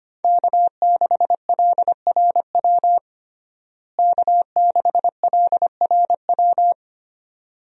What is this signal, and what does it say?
Morse code, heard as a keyed tone, about 25 wpm, 700 Hz. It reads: K6LRW K6LRW
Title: USA Call Signs - 5 Characters - Repeated 2x - 25wpm